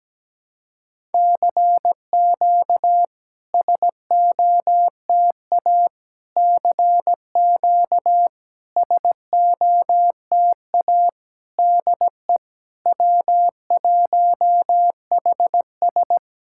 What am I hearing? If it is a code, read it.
Morse code, heard as a keyed tone, about 17 wpm, 690 Hz. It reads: CQ SOTA CQ SOTA DE W1HS
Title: SOTA QSO Practice 17wpm